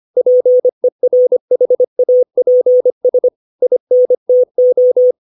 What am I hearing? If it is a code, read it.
Morse code, heard as a keyed tone, about 25 wpm, 500 Hz. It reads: PERHAPS INTO
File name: 632 words / 632 Words 25WPM